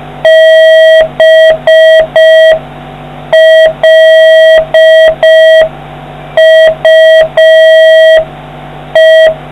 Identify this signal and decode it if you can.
Morse code, heard as a keyed tone, about 5 wpm, 1950 Hz, with noise. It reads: BLUE